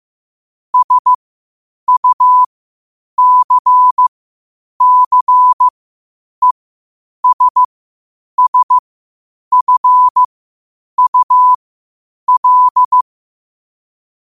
Morse code, heard as a keyed tone, about 15 wpm, spaced as for 8 wpm, 1000 Hz. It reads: SUCCESSFUL
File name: qotdfiles / 8wpm